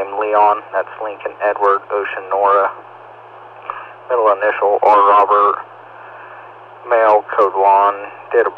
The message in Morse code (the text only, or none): none